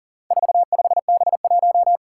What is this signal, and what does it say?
Morse code, heard as a keyed tone, about 40 wpm, 700 Hz. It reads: 45B1
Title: Four Challenging Letter-Number Combinations 40wpm